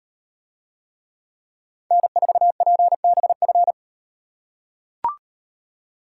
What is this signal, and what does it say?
Morse code, heard as a keyed tone, about 38 wpm, 700 Hz. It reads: N4PBF